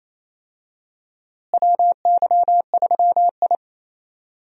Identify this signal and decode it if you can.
Morse code, heard as a keyed tone, about 28 wpm, 700 Hz. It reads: WY3I
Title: USA Call Signs - 4 Characters - Repeated 3x - 28wpm